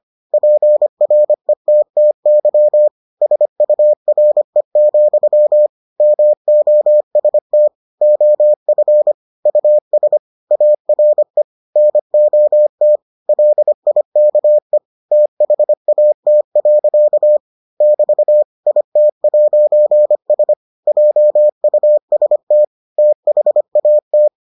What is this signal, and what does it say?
Morse code, heard as a keyed tone, about 25 wpm, 600 Hz. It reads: PRETTY SURE, MOST OF US ARE NOT LIKE THAT. <BT> IT'S JUST THAT